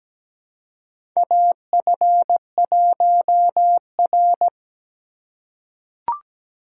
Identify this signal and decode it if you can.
Morse code, heard as a keyed tone, about 17 wpm, 700 Hz. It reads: AF1R
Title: USA Call Signs - 4 Characters - 4x Character Spacing - 17wpm